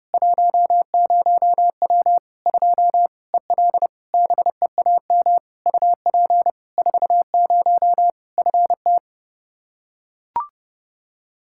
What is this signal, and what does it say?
Morse code, heard as a keyed tone, about 30 wpm, 700 Hz. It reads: 10W 2 EL BEAM UP 40 FT
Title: Practice QSO - Exchange 2 - Info - 30wpm